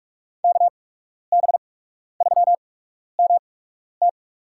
Morse code, heard as a keyed tone, about 46 wpm, spaced as for 12 wpm, 700 Hz. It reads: KB3MT
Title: USA Call Signs - 8x Character Spacing - 45wpm